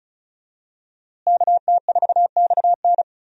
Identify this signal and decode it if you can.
Morse code, heard as a keyed tone, about 36 wpm, 700 Hz. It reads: KT4XN
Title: USA Call Signs - 5 Characters - Repeated 3x - 35wpm